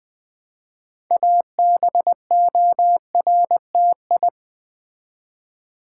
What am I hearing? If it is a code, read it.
Morse code, heard as a keyed tone, about 20 wpm, 700 Hz. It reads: ABORTI
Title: Build A Word - 2 Letters At A Time - 8-Letter Words - 20wpm